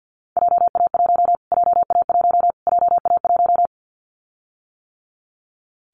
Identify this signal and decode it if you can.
Morse code, heard as a keyed tone, about 50 wpm, 700 Hz. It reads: JA1 JA1 JA1
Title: Callsign Prefixes for Top 25 DXCC - Repeated 3x - 50wpm